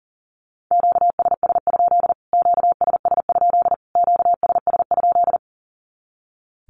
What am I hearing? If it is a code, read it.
Morse code, heard as a keyed tone, about 40 wpm, 700 Hz. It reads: QSS? QSS? QSS?
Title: Q-Codes Repeated 3x 40wpm